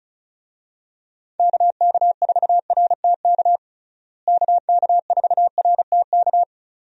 Morse code, heard as a keyed tone, about 36 wpm, 700 Hz. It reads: KK4RTK KK4RTK
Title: USA Call Signs - 6 Characters - Repeated 2x - 35wpm